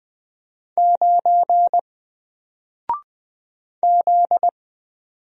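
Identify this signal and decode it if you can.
Morse code, heard as a keyed tone, about 20 wpm, 700 Hz. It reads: 9 Z
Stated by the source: Single letter and number Morse drill at 20 wpm, from N4LCD.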